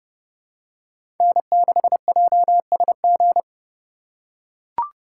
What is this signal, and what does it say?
Morse code, heard as a keyed tone, about 31 wpm, 700 Hz. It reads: N6JSG
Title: USA Call Signs - 5 Characters - 4x Character Spacing - 30wpm